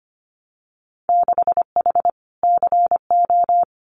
Morse code, heard as a keyed tone, about 25 wpm, 700 Hz. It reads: 6H CO
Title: Winter Field Day Exchanges Encore 25wpm